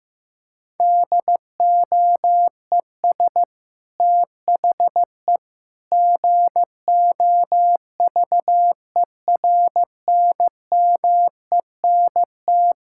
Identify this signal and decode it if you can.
Morse code, heard as a keyed tone, about 15 wpm, 700 Hz. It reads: DOES THE GOVERNMENT
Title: Sentences from Top 1600 Words 15wpm